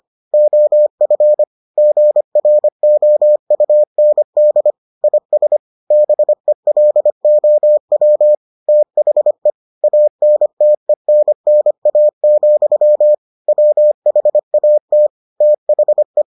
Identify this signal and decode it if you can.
Morse code, heard as a keyed tone, about 25 wpm, 600 Hz. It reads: OF GROUND IS BELOW THE ANTENNA, WHAT THE